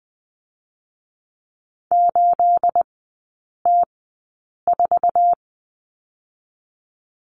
Morse code, heard as a keyed tone, about 20 wpm, spaced as for 8 wpm, 700 Hz. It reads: 8T4
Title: 218 Lesson 35e - Number 0 - Three Characters 20wpm